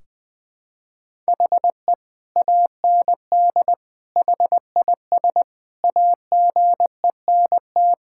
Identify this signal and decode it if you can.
Morse code, heard as a keyed tone, about 21 wpm, 700 Hz. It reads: HE AND HIS AGENT
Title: Sentences from Top 1600 Words 20wpm